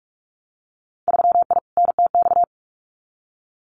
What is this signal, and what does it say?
Morse code, heard as a keyed tone, about 44 wpm, 700 Hz. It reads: 3I NTX
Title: Winter Field Day Exchanges 45wpm